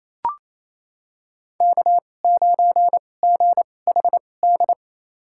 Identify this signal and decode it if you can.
Morse code, heard as a keyed tone, about 28 wpm, spaced as for 20 wpm, 700 Hz. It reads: K9GHD
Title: USA Call Signs - 5 Characters - 2x Character Spacing - 28wpm